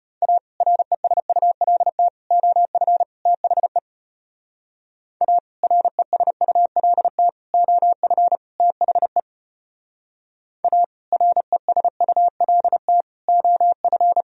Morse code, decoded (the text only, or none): A RESULT OF THE A RESULT OF THE A RESULT OF